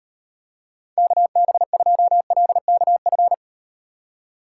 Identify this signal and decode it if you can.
Morse code, heard as a keyed tone, about 38 wpm, 700 Hz. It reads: KB2LKF